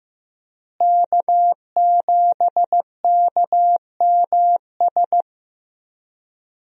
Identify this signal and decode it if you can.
Morse code, heard as a keyed tone, about 15 wpm, 700 Hz. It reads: K7KMS
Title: USA Call Signs - 5 Characters - Repeated 3x - 15wpm